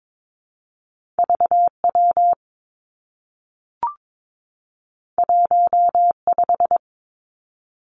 Morse code, heard as a keyed tone, about 22 wpm, 700 Hz. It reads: V W 1 5
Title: Two Challenging Letter-Number Combinations 22wpm